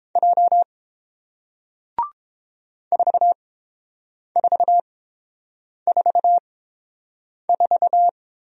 Morse code, heard as a keyed tone, about 30 wpm, spaced as for 7 wpm, 690 Hz. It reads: J 4444